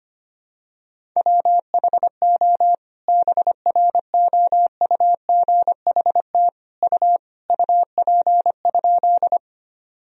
Morse code, heard as a keyed tone, about 25 wpm, 700 Hz. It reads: WHO BROUGHT U UP?
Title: Sentences from Top 500 Words and Abbreviations 25wpm